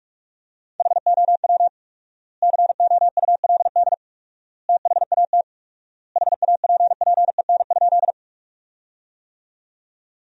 Morse code, heard as a keyed tone, about 46 wpm, 700 Hz. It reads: HOW COULD THAT HAPPEN?
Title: 4x Word-Spacing - Sentences from Top 400 Words - 45wpm